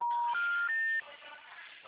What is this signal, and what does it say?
Special information tone, with noise.
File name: out- -2014- - - .74150